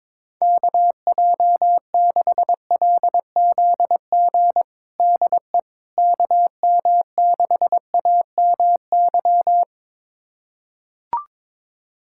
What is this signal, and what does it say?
Morse code, heard as a keyed tone, about 22 wpm, 700 Hz. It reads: KJ6LZG DE KM6AMY
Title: USA Callsign DE Callsign - 6 Characters - 22wpm